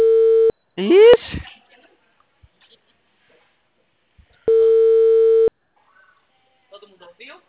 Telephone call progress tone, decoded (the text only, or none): ringback tone